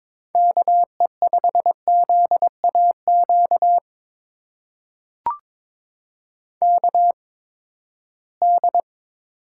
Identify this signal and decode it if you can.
Morse code, heard as a keyed tone, about 22 wpm, spaced as for 15 wpm, 700 Hz. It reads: KE5ZAQ KD
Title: USA Call Signs - 6 Characters - 8x Character Spacing - 22wpm